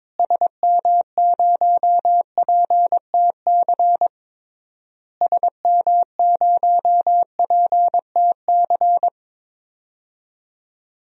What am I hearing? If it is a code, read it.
Morse code, heard as a keyed tone, about 22 wpm, 700 Hz. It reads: SM0PTC SM0PTC
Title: DXCC-100 Call Signs - 6 Characters - Repeated 3x - 22wpm